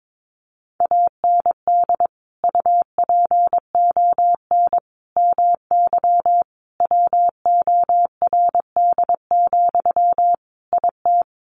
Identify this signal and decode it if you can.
Morse code, heard as a keyed tone, about 22 wpm, 700 Hz. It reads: AND UPON MY WORD, IT
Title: Sentences from Top 1100 Words 22wpm